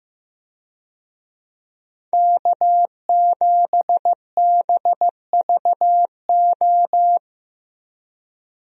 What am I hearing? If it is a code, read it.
Morse code, heard as a keyed tone, about 16 wpm, 700 Hz. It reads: K7BVO
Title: USA Call Signs - 5 Characters - Repeated 2x - 15wpm